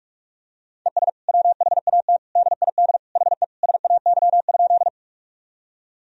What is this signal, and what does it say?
Morse code, heard as a keyed tone, about 47 wpm, 700 Hz. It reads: ES WHAT DID HE SAY?
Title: Sentences from Top 200 Words and Abbreviations 45wpm